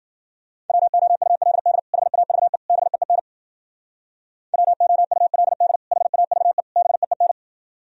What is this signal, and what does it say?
Morse code, heard as a keyed tone, about 52 wpm, 700 Hz. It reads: WOULD HAVE BEEN WOULD HAVE BEEN